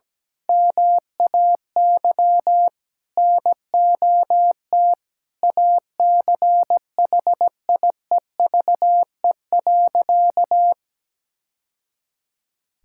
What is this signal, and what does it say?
Morse code, heard as a keyed tone, about 17 wpm, 700 Hz. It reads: MAY NOT ACHIEVE.
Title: Sentences from Top 1000 Words - Review Part 6 - 17wpm